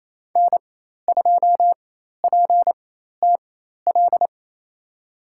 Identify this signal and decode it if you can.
Morse code, heard as a keyed tone, about 28 wpm, spaced as for 13 wpm, 700 Hz. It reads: N2PTL
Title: USA Call Signs - 5 Characters - 4x Character Spacing - 28wpm